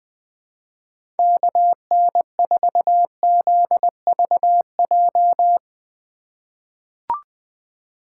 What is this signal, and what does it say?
Morse code, heard as a keyed tone, about 20 wpm, 700 Hz. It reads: KN4ZVJ
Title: 2023 POTA Hunter Call Signs - 20wpm